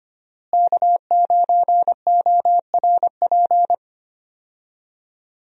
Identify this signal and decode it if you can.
Morse code, heard as a keyed tone, about 25 wpm, 700 Hz. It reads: K9ORP
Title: USA Call Signs - 5 Characters - Repeated 2x - 25wpm